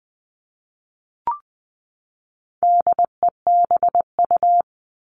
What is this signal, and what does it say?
Morse code, heard as a keyed tone, about 20 wpm, 700 Hz. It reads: DEBU
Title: All Cut Numbers - 4-Digit 20wpm